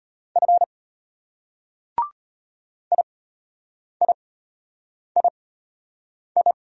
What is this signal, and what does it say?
Morse code, heard as a keyed tone, about 33 wpm, spaced as for 8 wpm, 690 Hz. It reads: F IIII